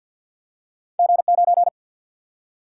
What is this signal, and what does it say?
Morse code, heard as a keyed tone, about 53 wpm, 700 Hz. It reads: G 9
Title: Single-Letter Single-Number 50wpm